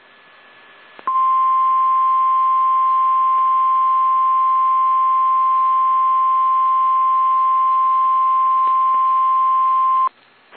Emergency Alert System (SAME) tones (the 1050 Hz attention tone). It is heard from 1.1 to 10.1 s.
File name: eas / KSDZKDJL / KSDZKDJL